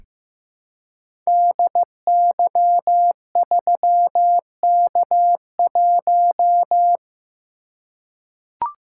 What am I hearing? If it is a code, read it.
Morse code, heard as a keyed tone, about 15 wpm, 700 Hz. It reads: DY3K1
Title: Code Groups - 5 Letter:Numbers - 15wpm